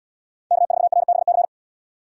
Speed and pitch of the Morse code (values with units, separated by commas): 77 wpm, 700 Hz